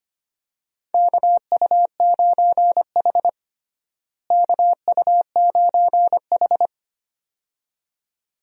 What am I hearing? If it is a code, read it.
Morse code, heard as a keyed tone, about 25 wpm, 700 Hz. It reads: KU9H KU9H
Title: USA Call Signs - 4 Characters - Repeated 2x - 25wpm